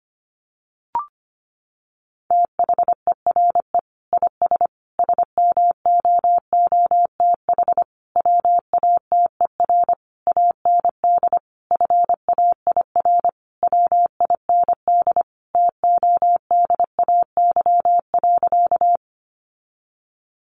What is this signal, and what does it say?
Morse code, heard as a keyed tone, about 25 wpm, 700 Hz. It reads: THERE IS SMOOTH WATER AND FAIR WIND TODAY.
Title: Sentences from Top 1400 Words 25wpm